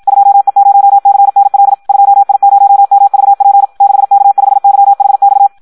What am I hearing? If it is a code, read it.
Morse code, heard as a keyed tone, about 52 wpm, 800 Hz. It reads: 2E0CNL 2I0MVP 6K5YHX
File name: Audio practica CW / qrq-tx-54 2